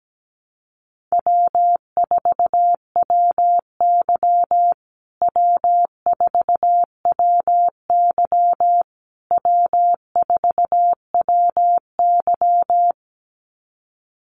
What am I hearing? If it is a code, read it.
Morse code, heard as a keyed tone, about 17 wpm, 700 Hz. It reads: W4WY W4WY W4WY
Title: USA Call Signs Repeated 3x 17wpm